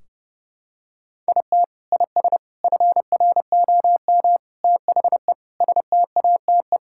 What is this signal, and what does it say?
Morse code, heard as a keyed tone, about 30 wpm, 700 Hz. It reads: IT IS FROM THE STATE
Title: Sentences from Top 1600 Words 30wpm